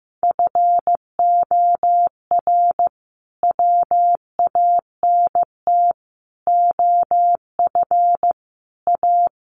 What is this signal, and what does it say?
Morse code, heard as a keyed tone, about 14 wpm, 700 Hz. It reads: FOR WANT OF A
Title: A Sound Of Thunder 15wpm